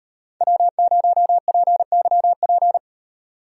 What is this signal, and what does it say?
Morse code, heard as a keyed tone, about 38 wpm, 700 Hz. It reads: W0PYP